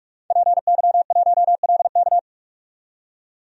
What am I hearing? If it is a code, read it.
Morse code, heard as a keyed tone, about 43 wpm, 700 Hz. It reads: PY1LK
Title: DXCC-100 Call Signs - 5 Characters - 45wpm